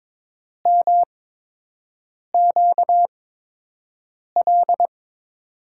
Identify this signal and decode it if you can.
Morse code, heard as a keyed tone, about 22 wpm, spaced as for 6 wpm, 700 Hz. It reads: MQL
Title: USA Call Signs - 6 Characters - 8x Character Spacing - 22wpm